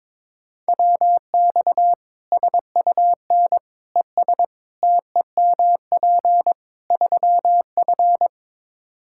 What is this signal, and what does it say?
Morse code, heard as a keyed tone, about 22 wpm, 700 Hz. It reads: WX SUN ES TEMP 3F
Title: K1USN SST QSO Practice 22wpm